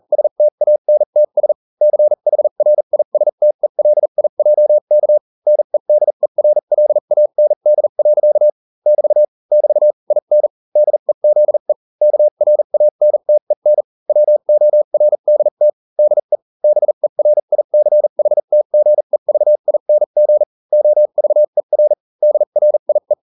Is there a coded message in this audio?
Yes: Morse code from 0.1 to 23.1 s, heard as a keyed tone, about 40 wpm, spaced as for 35 wpm, 600 Hz. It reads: STANTS CHRISTELIJK NEDERLAND. <BT> <BT> IN DEZE KRANTEN WORDT DE BERICHTGEVING OVER DRIE